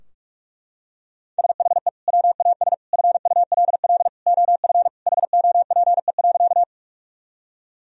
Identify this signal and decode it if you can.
Morse code, heard as a keyed tone, about 42 wpm, 700 Hz. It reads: SHE WAS FULL OF HOPE.
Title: Sentences from Top 1000 Words - Review Part 3 - 45wpm